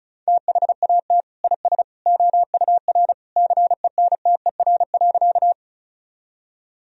Morse code, heard as a keyed tone, about 36 wpm, 700 Hz. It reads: THAT IS OUR CENTER.
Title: Sentences from Top 1000 Words - Review Part 2 - 35wpm